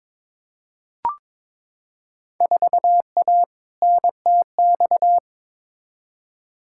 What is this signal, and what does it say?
Morse code, heard as a keyed tone, about 22 wpm, 700 Hz. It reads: 4A NTX
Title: Field Day Exchanges - 22wpm